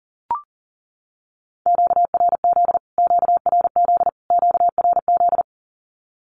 Morse code, heard as a keyed tone, about 40 wpm, 700 Hz. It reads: QRZ QRZ QRZ
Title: Q-Codes Repeated 3x 40wpm